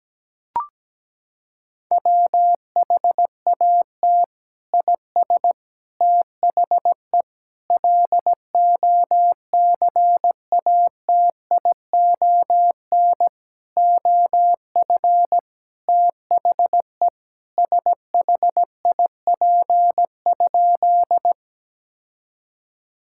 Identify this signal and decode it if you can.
Morse code, heard as a keyed tone, about 17 wpm, 700 Hz. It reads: WHAT IS THE LOCATION OF THE SHIP?
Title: Sentences from Top 1000 Words - Review Part 6 - 17wpm